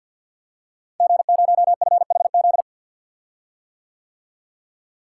Morse code, heard as a keyed tone, about 51 wpm, 700 Hz. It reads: G0FHZ
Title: Code Groups - 5 Letter:Numbers - 50wpm